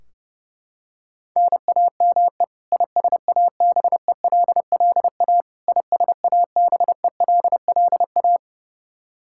Morse code, heard as a keyed tone, about 32 wpm, 700 Hz. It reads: NAME ISABELLA ISABELLA
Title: Practice QSO - Exchange 1 - Beginning - 30wpm